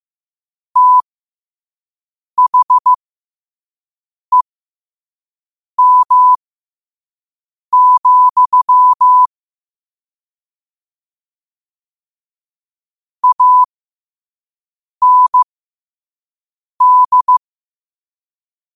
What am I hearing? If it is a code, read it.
Morse code, heard as a keyed tone, about 15 wpm, spaced as for 5 wpm, 1000 Hz. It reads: THEM, AND
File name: qotdfiles / 5wpm